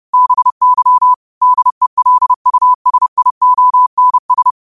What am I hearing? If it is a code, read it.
Morse code, heard as a keyed tone, about 30 wpm, 1000 Hz. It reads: BY DELUSIONS